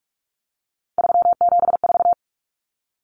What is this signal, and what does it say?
Morse code, heard as a keyed tone, about 44 wpm, 700 Hz. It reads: 374
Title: Numbers - 3-Digit 45wpm